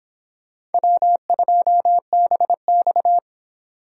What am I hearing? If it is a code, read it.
Morse code, heard as a keyed tone, about 26 wpm, 700 Hz. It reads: W2BX